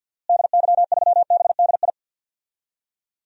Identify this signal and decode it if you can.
Morse code, heard as a keyed tone, about 52 wpm, 700 Hz. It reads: DY3BDI
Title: DXCC-100 Call Signs - 6 Characters - Repeated 3x - 50wpm